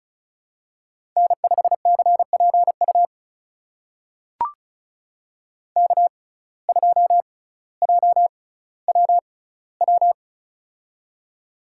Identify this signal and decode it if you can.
Morse code, heard as a keyed tone, about 35 wpm, spaced as for 18 wpm, 700 Hz. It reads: N5CPU K2JWW
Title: USA Call Signs - 5 Characters - 6x Character Spacing - 35wpm